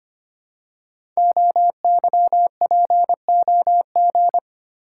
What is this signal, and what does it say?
Morse code, heard as a keyed tone, about 25 wpm, 700 Hz. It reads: OYPOG
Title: Code Groups - 5 Letters - 25wpm